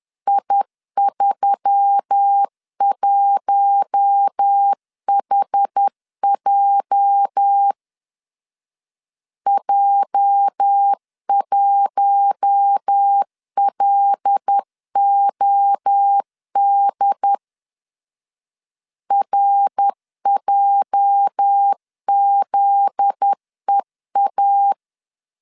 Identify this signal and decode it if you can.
Morse code, heard as a keyed tone, about 10 wpm, 800 Hz. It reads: I31HJ J1LOD RJZEA
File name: LEZIONI / 8 YJ45 / 8 YJ 10wpm 2651